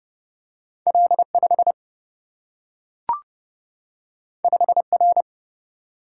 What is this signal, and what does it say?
Morse code, heard as a keyed tone, about 32 wpm, 700 Hz. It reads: L 5 5 R
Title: Single-Letter Single-Number 30wpm